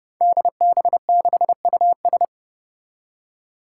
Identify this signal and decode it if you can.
Morse code, heard as a keyed tone, about 30 wpm, 700 Hz. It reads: DB6US
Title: Five Challenging Letter-Number Combinations 30wpm